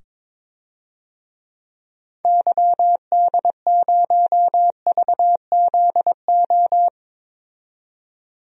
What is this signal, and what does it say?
Morse code, heard as a keyed tone, about 22 wpm, 700 Hz. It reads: YD0VZO
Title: DXCC-100 Call Signs - 6 Characters - 22wpm